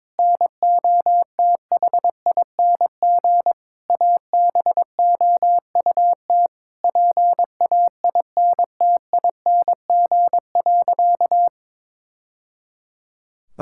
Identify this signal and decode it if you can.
Morse code, heard as a keyed tone, about 22 wpm, 700 Hz. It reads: NOTHING ABOUT PAINTING.
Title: Sentences from Top 1000 Words - Review Part 6 - 22wpm